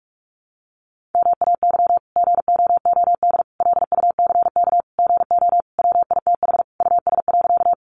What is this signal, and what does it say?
Morse code, heard as a keyed tone, about 44 wpm, 700 Hz. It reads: MAY GOOD LUCK GO WITH US.
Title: Sentences from Top 1500 Words 45wpm